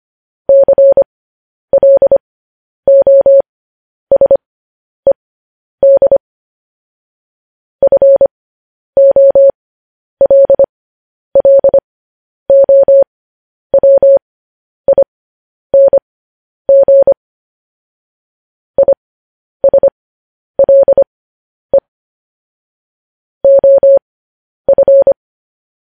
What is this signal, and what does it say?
Morse code, heard as a keyed tone, about 25 wpm, spaced as for 10 wpm, 550 Hz. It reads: CLOSED FOLLOWING ISLE OF